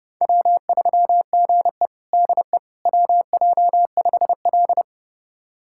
Morse code, heard as a keyed tone, about 30 wpm, 700 Hz. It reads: W3GE DE WJ5L
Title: USA Callsign DE Callsign - 4 Characters - 30wpm